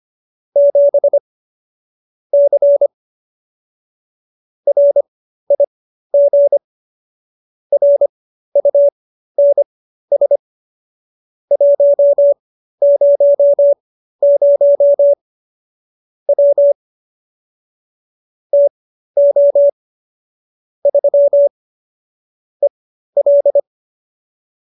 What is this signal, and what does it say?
Morse code, heard as a keyed tone, about 25 wpm, spaced as for 13 wpm, 570 Hz. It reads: 7 C RIG RUNS 100 W TO 3 EL
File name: qso208b 13